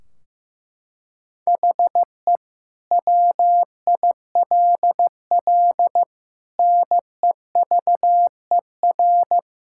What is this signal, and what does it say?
Morse code, heard as a keyed tone, about 16 wpm, 700 Hz. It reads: HE WILL NEVER
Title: Sentences from Top 1600 Words 15wpm